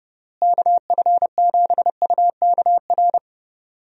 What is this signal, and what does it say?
Morse code, heard as a keyed tone, about 32 wpm, 700 Hz. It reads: KF7UKR
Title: USA Call Signs - 6 Characters - Repeated 2x - 30wpm